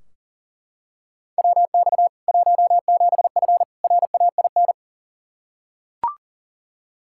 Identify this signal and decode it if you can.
Morse code, heard as a keyed tone, about 40 wpm, 700 Hz. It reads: WX 17F RAIN
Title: K1USN SST QSO Practice 40wpm